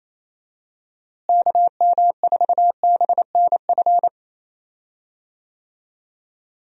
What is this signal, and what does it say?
Morse code, heard as a keyed tone, about 27 wpm, 700 Hz. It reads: KM4BNF
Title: USA Call Signs - 6 Characters - Repeated 3x - 28wpm